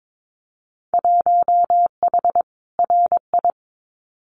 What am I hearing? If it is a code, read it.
Morse code, heard as a keyed tone, about 22 wpm, 700 Hz. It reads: 1H RI
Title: Winter Field Day Exchanges 22wpm